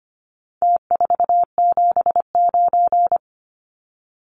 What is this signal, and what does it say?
Morse code, heard as a keyed tone, about 25 wpm, 700 Hz. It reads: T479
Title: Numbers - Contest Style - 4-Digit 25wpm